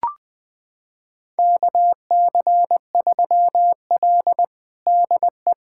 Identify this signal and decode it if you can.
Morse code, heard as a keyed tone, about 20 wpm, 700 Hz. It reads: KC3L DE
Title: USA Callsign DE Callsign - 4 Characters - 20wpm